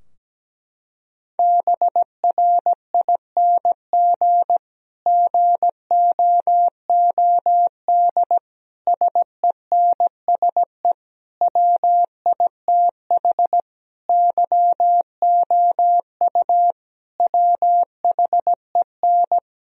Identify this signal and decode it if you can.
Morse code, heard as a keyed tone, about 17 wpm, 700 Hz. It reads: BRING GOOD SENSE WITH YOU WHEN
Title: Sentences from Top 1000 Words - Review Part 4 - 17wpm